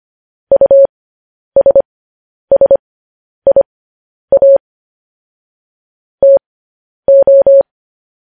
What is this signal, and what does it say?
Morse code, heard as a keyed tone, about 25 wpm, spaced as for 10 wpm, 550 Hz. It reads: USSIA TO
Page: news-headlines-in-morse-code-at-10-wpm